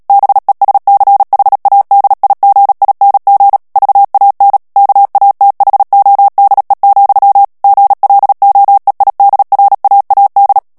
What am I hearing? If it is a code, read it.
Morse code, heard as a keyed tone, about 37 wpm, 800 Hz. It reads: BESCHADIGING VAN KATHODE, GLOEIDRAAD
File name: morse mp3 / HOLLANDAIS texte clair 2 vitesse 35 mots minute